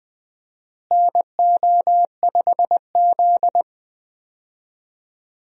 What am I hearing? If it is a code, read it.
Morse code, heard as a keyed tone, about 20 wpm, 700 Hz. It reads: NO5Z
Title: USA Call Signs - 4 Characters - Repeated 3x - 20wpm